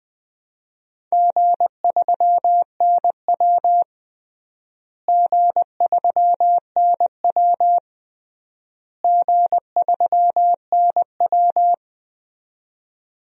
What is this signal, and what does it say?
Morse code, heard as a keyed tone, about 21 wpm, 700 Hz. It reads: G3NW G3NW G3NW
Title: DXCC-100 Call Signs - 4 Characters - Repeated 3x - 20wpm